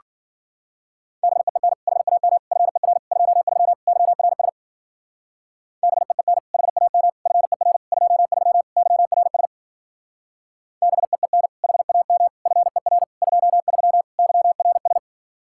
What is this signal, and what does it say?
Morse code, heard as a keyed tone, about 49 wpm, 700 Hz. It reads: BEEN HAM FER 23 YRS BEEN HAM FER 23 YRS BEEN HAM FER 23 YRS
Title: Speed-Racing Common QSO Phrases - 40wpm